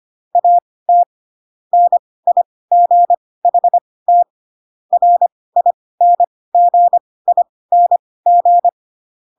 Morse code, heard as a keyed tone, about 25 wpm, spaced as for 18 wpm, 700 Hz. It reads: AT NIGHT RINGING
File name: Pinocchio / Pinocchio ch6 12 25 ik1hge